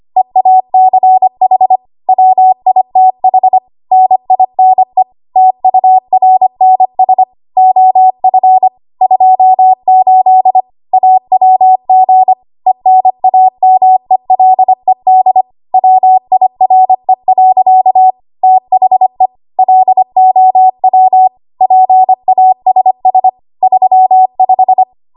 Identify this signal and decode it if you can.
Morse code, heard as a keyed tone, about 25 wpm, 750 Hz. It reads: EACH WITH NINE TURNS OF 28 AWG ENAMELED WIRE. THE LOW PASS 35
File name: Archive / 25 WPM / 25WPM